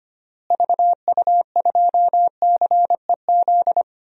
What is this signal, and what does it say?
Morse code, heard as a keyed tone, about 25 wpm, 700 Hz. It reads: VU2CEZ
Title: DXCC-100 Call Signs - 6 Characters - Repeated 2x - 25wpm